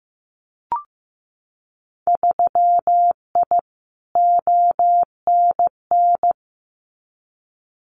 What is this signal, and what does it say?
Morse code, heard as a keyed tone, about 15 wpm, 700 Hz. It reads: 3I ONN
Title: Winter Field Day Exchanges Encore 15wpm